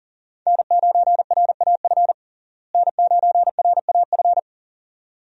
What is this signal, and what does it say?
Morse code, heard as a keyed tone, about 37 wpm, 700 Hz. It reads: N9RAF N9RAF
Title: USA Call Signs - 5 Characters - Repeated 2x - 40wpm